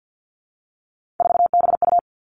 Morse code, heard as a keyed tone, about 50 wpm, 700 Hz. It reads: 4BU
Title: Three Challenging Letter-Number Combinations 50wpm